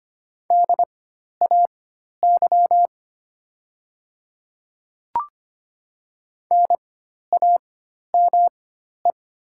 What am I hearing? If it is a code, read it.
Morse code, heard as a keyed tone, about 25 wpm, spaced as for 12 wpm, 700 Hz. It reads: DAY NAME
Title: 4x Character Spacing - Top 500 Words - 25wpm